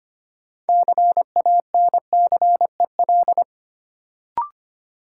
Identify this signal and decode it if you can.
Morse code, heard as a keyed tone, about 25 wpm, 700 Hz. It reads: CANCEL
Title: Sentences from Top 1200 Words - New Word Drill Alt - 25wpm